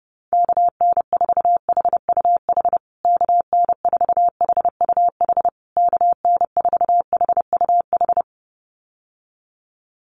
Morse code, heard as a keyed tone, about 30 wpm, 700 Hz. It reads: KN4HUH KN4HUH KN4HUH
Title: USA Call Signs Repeated 3x 30wpm